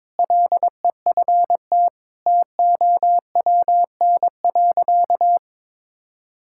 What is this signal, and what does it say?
Morse code, heard as a keyed tone, about 22 wpm, 700 Hz. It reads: LEFT TOWN.
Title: Sentences from Top 1000 Words - Review Part 6 - 22wpm